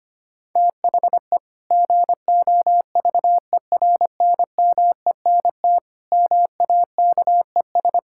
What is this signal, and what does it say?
Morse code, heard as a keyed tone, about 25 wpm, 700 Hz. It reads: THE GOVERNMENT MAKES
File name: practice / 25wpm / Sentences with 2-Word Phrases - 25wpm